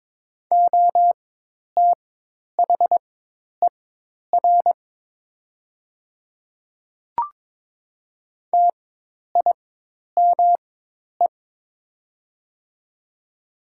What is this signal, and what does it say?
Morse code, heard as a keyed tone, about 22 wpm, spaced as for 10 wpm, 700 Hz. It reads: OTHER TIME
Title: 4x Character Spacing - Top 100 Words - 22wpm